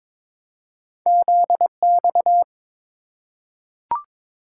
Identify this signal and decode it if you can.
Morse code, heard as a keyed tone, about 22 wpm, 700 Hz. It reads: ZX
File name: practice / 22wpm / Two Letter-Number Combinations - 22wpm